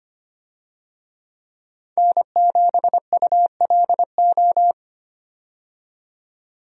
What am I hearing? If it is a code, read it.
Morse code, heard as a keyed tone, about 25 wpm, 700 Hz. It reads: N7ULO
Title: USA Call Signs - 5 Characters - 25wpm